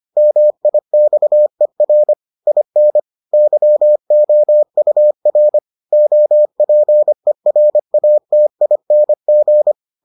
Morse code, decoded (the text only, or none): MIXER IN YOUR OPERATING